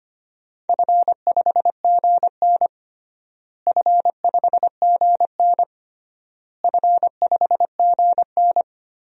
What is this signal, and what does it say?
Morse code, heard as a keyed tone, about 25 wpm, 700 Hz. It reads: F5GN F5GN F5GN
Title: DXCC-100 Call Signs - 4 Characters - Repeated 3x - 25wpm